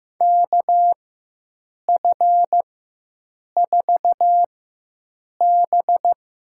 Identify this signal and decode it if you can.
Morse code, heard as a keyed tone, about 15 wpm, spaced as for 7 wpm, 700 Hz. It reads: KF4B
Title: USA Call Signs - 6 Characters - 4x Character Spacing - 15wpm